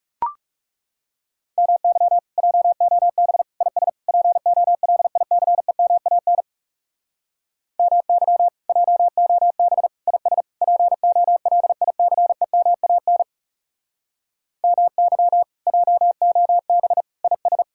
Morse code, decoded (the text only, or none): MY JOB IS POLICEMAN MY JOB IS POLICEMAN MY JOB IS